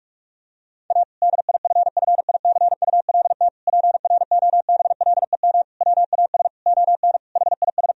Morse code, heard as a keyed tone, about 46 wpm, 700 Hz. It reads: A DIFFICULT PROBLEM WAS ON HIS
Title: Sentences from Top 1000 Words - Review Part 2 - 45wpm